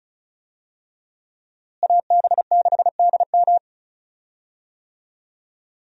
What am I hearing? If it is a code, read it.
Morse code, heard as a keyed tone, about 36 wpm, 700 Hz. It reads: AB6DM
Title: USA Call Signs - 5 Characters - Repeated 2x - 35wpm